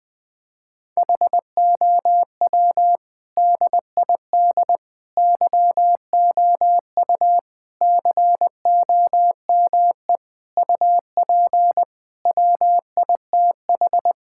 Morse code, heard as a keyed tone, about 20 wpm, 700 Hz. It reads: HOW DID YOU COME UP WITH
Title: Sentences with 7-Word Phrases - 20wpm